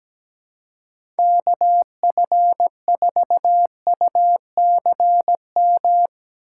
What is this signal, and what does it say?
Morse code, heard as a keyed tone, about 17 wpm, 700 Hz. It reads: KF4UCM
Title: USA Call Signs - 4x Character Spacing - 17wpm